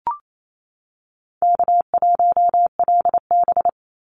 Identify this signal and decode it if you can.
Morse code, heard as a keyed tone, about 28 wpm, 700 Hz. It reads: K1LB
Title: NA SOTA Chaser Callsigns 28wpm